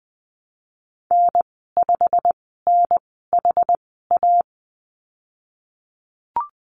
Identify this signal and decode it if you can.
Morse code, heard as a keyed tone, about 20 wpm, spaced as for 15 wpm, 700 Hz. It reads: N5NHA
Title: 167 Lesson 27g - Number 7 - Call Signs 20wpm